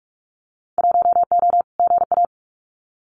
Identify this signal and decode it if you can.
Morse code, heard as a keyed tone, about 44 wpm, 700 Hz. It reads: 1O GA
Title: Winter Field Day Exchanges 45wpm